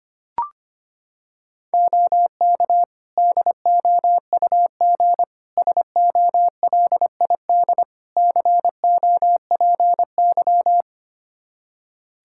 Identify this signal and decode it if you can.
Morse code, heard as a keyed tone, about 25 wpm, 700 Hz. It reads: OK DOUG SOLID COPY
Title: Practice QSO - Exchange 3 - More Info - 25wpm